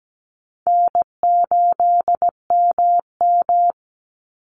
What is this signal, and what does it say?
Morse code, heard as a keyed tone, about 17 wpm, 700 Hz. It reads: N8MM
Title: USA Call Signs 17wpm